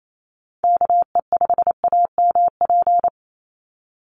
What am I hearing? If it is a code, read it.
Morse code, heard as a keyed tone, about 28 wpm, 700 Hz. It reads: KE5AMP
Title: USA Call Signs 28wpm